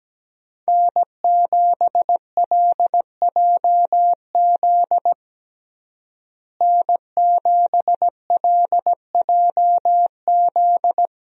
Morse code, heard as a keyed tone, about 17 wpm, 700 Hz. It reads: N7LJZ N7LJZ
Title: USA Call Signs - 5 Characters - Repeated 2x - 17wpm